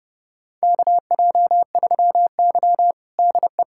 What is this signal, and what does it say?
Morse code, heard as a keyed tone, about 32 wpm, 700 Hz. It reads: KJ3Y DE
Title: USA Callsign DE Callsign - 4 Characters - 30wpm